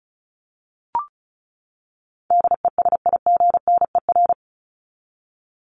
Morse code, heard as a keyed tone, about 34 wpm, 700 Hz. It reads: DESIGNER
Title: Top 1501-1750 Words 35wpm